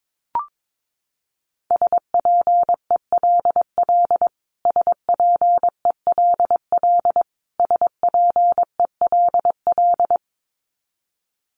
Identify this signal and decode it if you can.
Morse code, heard as a keyed tone, about 22 wpm, 700 Hz. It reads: SPELL SPELL SPELL
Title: Top 300 Words Repeated 22wpm